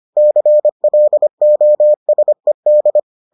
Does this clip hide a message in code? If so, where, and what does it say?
Yes: Morse code from 0.2 to 3.0 s, heard as a keyed tone, about 25 wpm, 600 Hz. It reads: CLOSED